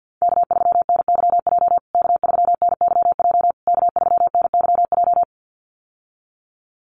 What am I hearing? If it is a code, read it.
Morse code, heard as a keyed tone, about 48 wpm, 700 Hz. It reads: K3NYJ K3NYJ K3NYJ
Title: USA Call Signs Repeated 3x 50wpm